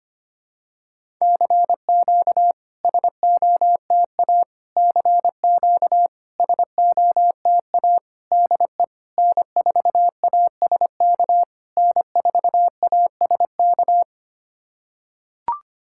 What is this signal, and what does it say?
Morse code, heard as a keyed tone, about 25 wpm, 690 Hz. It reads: CQ SOTA CQ SOTA DE N4ASK N4ASK
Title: SOTA QSO Practice 25wpm